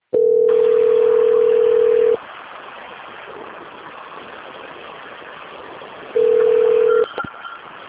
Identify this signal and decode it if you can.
Ringback tone (North America), 440 + 480 Hz.
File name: out- - - - - .24699